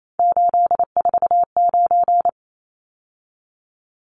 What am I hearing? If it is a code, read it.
Morse code, heard as a keyed tone, about 28 wpm, 700 Hz. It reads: 849
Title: Numbers - 3-Digit 28wpm